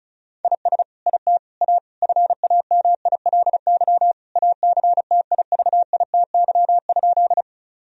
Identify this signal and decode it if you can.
Morse code, heard as a keyed tone, about 36 wpm, 700 Hz. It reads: IS IT A FAMILY ACTIVITY?
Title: Sentences from Top 1000 Words - Review Part 4 - 35wpm